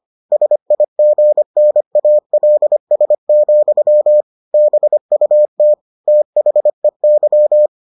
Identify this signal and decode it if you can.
Morse code, heard as a keyed tone, about 25 wpm, 600 Hz. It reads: SIGNALS, BUT THEY